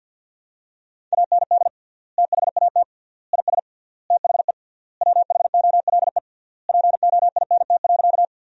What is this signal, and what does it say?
Morse code, heard as a keyed tone, about 47 wpm, 700 Hz. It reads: AND THAT IS THE WHOLE POINT.
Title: 3x Word-Spacing - Sentences from Top 400 Words - 50wpm